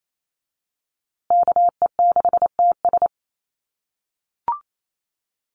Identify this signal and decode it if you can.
Morse code, heard as a keyed tone, about 28 wpm, 700 Hz. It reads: KE6TS
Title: USA Call Signs Repeated 3x 28wpm